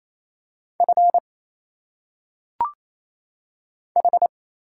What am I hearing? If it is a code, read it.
Morse code, heard as a keyed tone, about 27 wpm, 700 Hz. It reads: F H
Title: Single Letters - Phonetic Pronunciation 28wpm